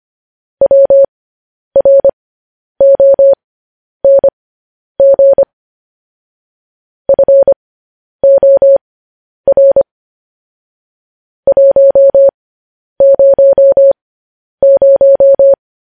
Morse code, heard as a keyed tone, about 25 wpm, spaced as for 10 wpm, 550 Hz. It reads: WRONG FOR 100
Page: news headlines in morse code at 10 wpm